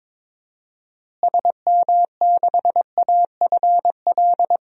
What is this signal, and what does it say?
Morse code, heard as a keyed tone, about 22 wpm, 700 Hz. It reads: SM6AFL
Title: DXCC-100 Call Signs - 6 Characters - Repeated 3x - 22wpm